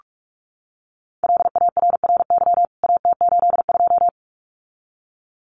Morse code, heard as a keyed tone, about 44 wpm, 700 Hz. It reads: LARRY AT82
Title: CWT Exchange 45wpm